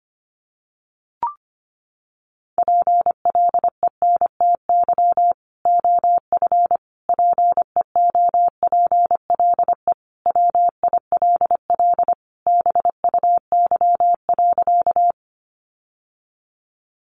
Morse code, heard as a keyed tone, about 25 wpm, 700 Hz. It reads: PLENTY OF PEOPLE WILL BUY.
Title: Sentences from Top 1300 Words 25wpm